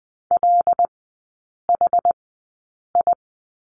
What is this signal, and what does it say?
Morse code, heard as a keyed tone, about 20 wpm, spaced as for 8 wpm, 700 Hz. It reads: LHI
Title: 100 Lesson 17e - Letter u - Three Characters 20wpm